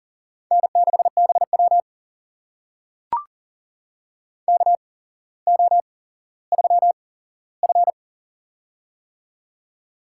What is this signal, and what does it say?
Morse code, heard as a keyed tone, about 40 wpm, spaced as for 17 wpm, 700 Hz. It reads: N6BW KO3F
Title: USA Call Signs - 4 Characters - 8x Character Spacing - 40wpm